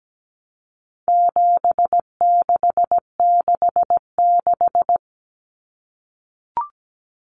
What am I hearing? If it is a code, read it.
Morse code, heard as a keyed tone, about 17 wpm, 700 Hz. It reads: 7666
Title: Numbers - 4-Digit 17wpm